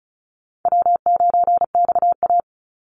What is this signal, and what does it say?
Morse code, heard as a keyed tone, about 34 wpm, 700 Hz. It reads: W9XA
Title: Call Signs Repeated 3x - 35wpm